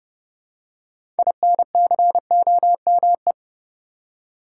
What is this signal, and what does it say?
Morse code, heard as a keyed tone, about 30 wpm, 700 Hz. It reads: INCOME
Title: Six Letter Words - 30wpm